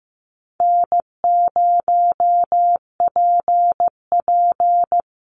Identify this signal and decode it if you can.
Morse code, heard as a keyed tone, about 15 wpm, 700 Hz. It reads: N0PP
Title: Easy Call Signs Repeated 3x - 15wpm